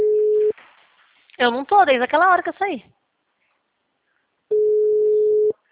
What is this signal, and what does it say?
Ringback tone (Europe, 425 Hz), with speech.